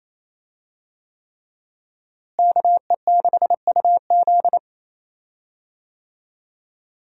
Morse code, heard as a keyed tone, about 28 wpm, 700 Hz. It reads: KE6UZ
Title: USA Call Signs - 5 Characters - Repeated 3x - 28wpm